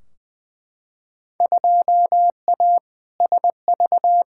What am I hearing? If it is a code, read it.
Morse code, heard as a keyed tone, about 21 wpm, 700 Hz. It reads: 2A SV
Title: Field Day Exchanges - 20wpm